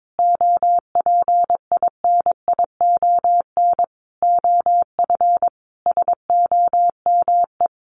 Morse code, heard as a keyed tone, about 22 wpm, 700 Hz. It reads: OPINION OF SOME
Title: Sentences from Top 1300 Words 22wpm